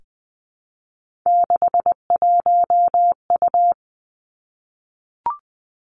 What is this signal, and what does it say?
Morse code, heard as a keyed tone, about 20 wpm, 700 Hz. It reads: 61U
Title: Three Challenging Letter-Number Combinations 20wpm